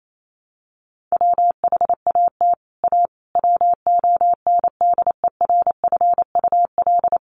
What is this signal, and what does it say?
Morse code, heard as a keyed tone, about 28 wpm, 700 Hz. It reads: WHAT A WONDERFUL
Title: Sentences from Top 1100 Words 28wpm